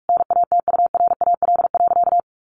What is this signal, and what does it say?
Morse code, heard as a keyed tone, about 44 wpm, 700 Hz. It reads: NATURAL.
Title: Sentences from Top 700 Words 45wpm